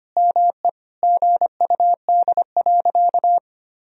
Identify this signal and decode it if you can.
Morse code, heard as a keyed tone, about 25 wpm, 700 Hz. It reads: ME GUD.
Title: Sentences from Top 500 Words and Abbreviations 25wpm